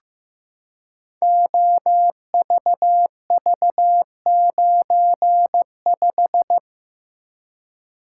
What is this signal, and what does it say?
Morse code, heard as a keyed tone, about 15 wpm, 700 Hz. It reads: OVV95
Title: Code Groups - 5 Letter:Numbers - 15wpm